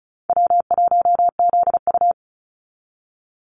Morse code, heard as a keyed tone, about 36 wpm, 700 Hz. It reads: W1ZU
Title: NA SOTA Activator Callsigns 35wpm